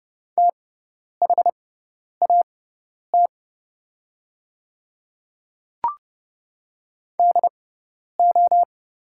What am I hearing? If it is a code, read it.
Morse code, heard as a keyed tone, about 30 wpm, spaced as for 10 wpm, 700 Hz. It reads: THAT DO
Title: 6x Character Spacing - Top 100 Words - 30wpm